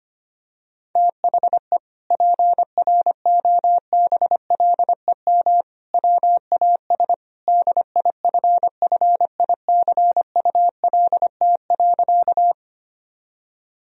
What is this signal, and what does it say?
Morse code, heard as a keyed tone, about 25 wpm, 700 Hz. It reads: THE PROBLEM WAS DIFFICULT.
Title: Sentences from Top 1000 Words - Review Part 2 - 25wpm